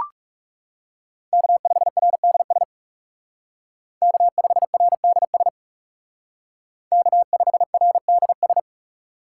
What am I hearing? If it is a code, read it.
Morse code, heard as a keyed tone, about 37 wpm, 700 Hz. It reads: K5RDS K5RDS K5RDS
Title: Speed Racing - USA 1x3 Call Signs 30wpm